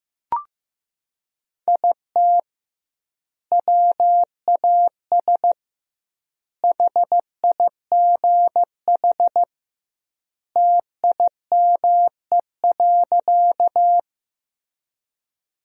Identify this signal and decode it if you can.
Morse code, heard as a keyed tone, about 16 wpm, 700 Hz. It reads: IT WAS HIGH TIME.
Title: 2x Word-Spacing - Sentences from Top 200 Words - 15wpm